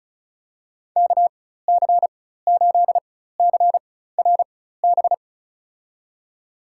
Morse code, heard as a keyed tone, about 34 wpm, spaced as for 16 wpm, 700 Hz. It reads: KC8CRB
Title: USA Call Signs - 6 Characters - 4x Character Spacing - 35wpm